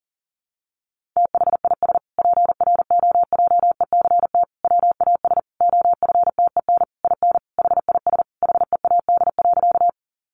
Morse code, heard as a keyed tone, about 40 wpm, 700 Hz. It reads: THIS PROJECT WAS OFTEN IN HIS HEAD.
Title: Sentences from Top 1400 Words 40wpm